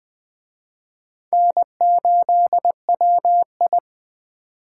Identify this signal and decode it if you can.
Morse code, heard as a keyed tone, about 20 wpm, 700 Hz. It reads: N8WI
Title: Speed Racing - USA 1x2 Call Signs 17wpm